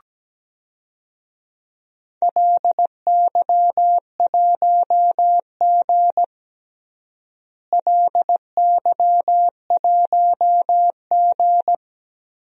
Morse code, heard as a keyed tone, about 17 wpm, 700 Hz. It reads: LY1G LY1G
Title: DXCC-100 Call Signs - 4 Characters - Repeated 3x - 17wpm